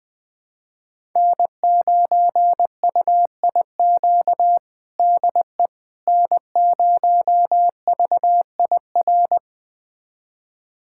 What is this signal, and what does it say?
Morse code, heard as a keyed tone, about 20 wpm, 700 Hz. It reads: N9UIQ DE N0VIR
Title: USA Callsign DE Callsign - 5 Characters - 20wpm